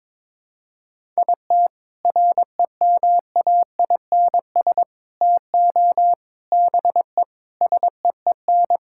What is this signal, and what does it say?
Morse code, heard as a keyed tone, about 22 wpm, 700 Hz. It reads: IT REMAINS TO BE SEEN
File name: practice / 22wpm / Sets of 8 Words - 22wpm